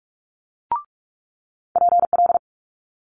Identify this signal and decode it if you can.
Morse code, heard as a keyed tone, about 46 wpm, 700 Hz. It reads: PL
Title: Multi-letter Phonemes Quick Fire 45wpm